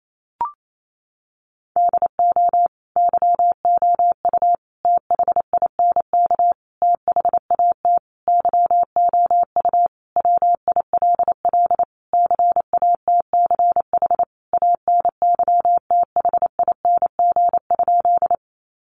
Morse code, heard as a keyed tone, about 28 wpm, 700 Hz. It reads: DO YOU THINK THAT YOU WILL CATCH ANYTHING?
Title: Sentences from Top 1500 Words 28wpm